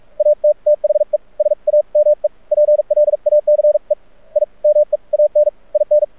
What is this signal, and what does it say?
Morse code, heard as a keyed tone, about 44 wpm, spaced as for 38 wpm, 600 Hz. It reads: AT THE SAME PLACE I MEAN IN
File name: archived / CRSnet